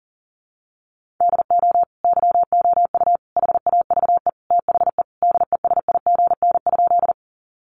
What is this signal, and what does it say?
Morse code, heard as a keyed tone, about 40 wpm, 690 Hz. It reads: DO YOU HAVE THE DESIGN?
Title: Sentences from Top 800 Words 40wpm